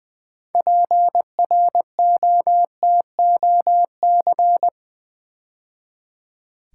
Morse code, heard as a keyed tone, about 20 wpm, 700 Hz. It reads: PROTOC
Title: Build A Word - 2 Letters At A Time - 8-Letter Words - 20wpm